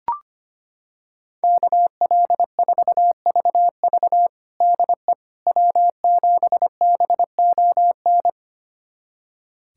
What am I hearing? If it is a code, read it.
Morse code, heard as a keyed tone, about 25 wpm, 700 Hz. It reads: KL4VV DE W7BON
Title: USA Callsign DE Callsign - 5 Characters - 25wpm